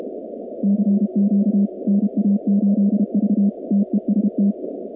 Morse code, heard as a keyed tone, about 32 wpm, 200 Hz. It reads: CQ NA8V TEST
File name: contests / 2019-cw-wpx-kt5j-k5pi- -2330 0-47